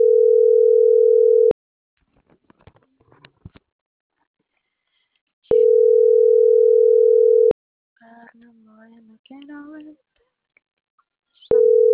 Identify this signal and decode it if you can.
Ringback tone (North America), 440 + 480 Hz.